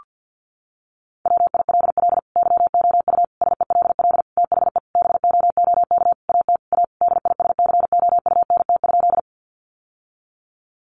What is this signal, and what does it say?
Morse code, heard as a keyed tone, about 50 wpm, 700 Hz. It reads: WILL YOU SELL THE BOOK AT A DISCOUNT?
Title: Sentences from Top 1200 Words 50wpm